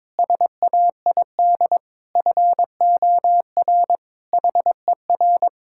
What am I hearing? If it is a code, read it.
Morse code, heard as a keyed tone, about 22 wpm, 700 Hz. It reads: SAID FOR HER
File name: practice / 22wpm / Sentences with 8-Word Phrases - 22wpm